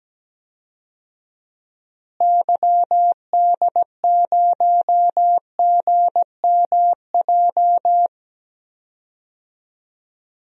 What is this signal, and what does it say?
Morse code, heard as a keyed tone, about 17 wpm, 700 Hz. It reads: YD0GMJ
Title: DXCC-100 Call Signs - 6 Characters - Repeated 2x - 17wpm